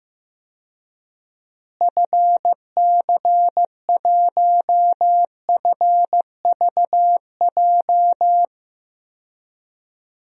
Morse code, heard as a keyed tone, about 15 wpm, 700 Hz. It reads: FC1FVJ
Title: DXCC-100 Call Signs - 6 Characters - Repeated 2x - 15wpm